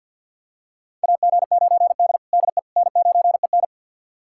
Morse code, heard as a keyed tone, about 50 wpm, 700 Hz. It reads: AG9D DE N9EN
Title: USA Callsign DE Callsign - 4 Characters - 50wpm